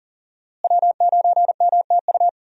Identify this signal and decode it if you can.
Morse code, heard as a keyed tone, about 40 wpm, 700 Hz. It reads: W9MTU
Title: USA Call Signs - 5 Characters - Repeated 3x - 40wpm